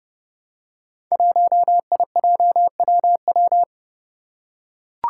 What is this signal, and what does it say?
Morse code, heard as a keyed tone, about 30 wpm, 700 Hz. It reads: 1IJWW
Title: Five Challenging Letter-Number Combinations 30wpm